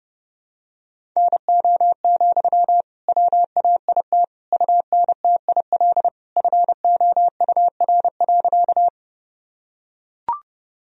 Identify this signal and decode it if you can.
Morse code, heard as a keyed tone, about 32 wpm, 700 Hz. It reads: NO, WAIT UNTIL FOUR.
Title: Sentences from Top 1000 Words - Review Part 1 - 30wpm